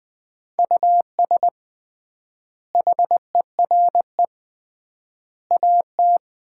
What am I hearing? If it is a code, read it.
Morse code, heard as a keyed tone, about 20 wpm, 700 Hz. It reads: US HERE AT
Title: 3x Word-Spacing - Sets of 5 Words - 20wpm